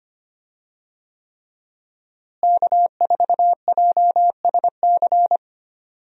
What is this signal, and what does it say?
Morse code, heard as a keyed tone, about 25 wpm, 700 Hz. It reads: K4JSC
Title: USA Call Signs - 5 Characters - Repeated 3x - 25wpm